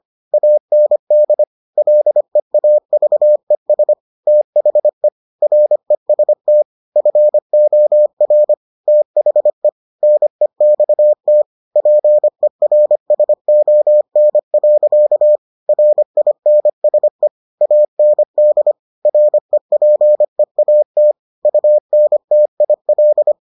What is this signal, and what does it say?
Morse code, heard as a keyed tone, about 25 wpm, 600 Hz. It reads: AND LEAVES THE REST FOR THE NEXT PERSON. RINSE AND REPEAT UNTIL